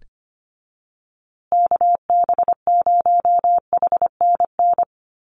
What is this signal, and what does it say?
Morse code, heard as a keyed tone, about 25 wpm, 700 Hz. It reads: KB0HNN
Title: USA Call Signs Repeated 2x 25wpm